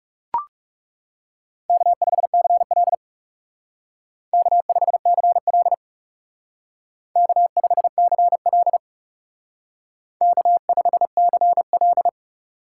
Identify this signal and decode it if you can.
Morse code, heard as a keyed tone, about 36 wpm, 700 Hz. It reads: K5CL K5CL K5CL K5CL
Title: Speed Racing - USA 1x2 Call Signs 30wpm